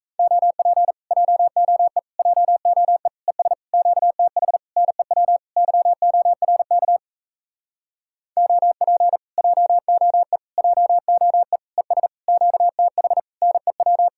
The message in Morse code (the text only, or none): OP JOE JOE ES QTH NEW YORK OP JOE JOE ES QTH NEW